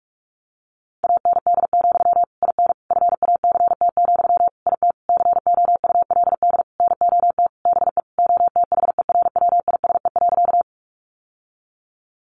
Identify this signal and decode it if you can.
Morse code, heard as a keyed tone, about 44 wpm, 700 Hz. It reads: AND, IN FACT, IT COULD NOT BE OTHERWISE.
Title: Sentences from Top 1200 Words 45wpm